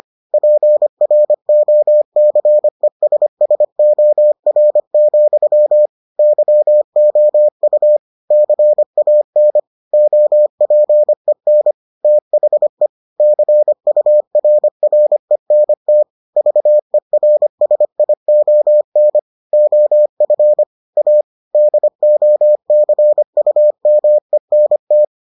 Morse code, heard as a keyed tone, about 25 wpm, 600 Hz. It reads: PROCESSOR, YOU CAN OPEN THE CURRENT VERSION OF A DOCUMENT